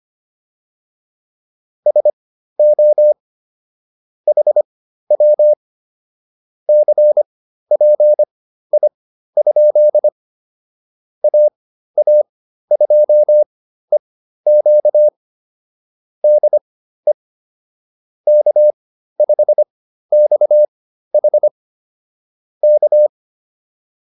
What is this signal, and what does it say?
Morse code, heard as a keyed tone, about 25 wpm, spaced as for 13 wpm, 600 Hz. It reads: SO HW CPI? AA2EQ DE K5XH K
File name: QSO 210 13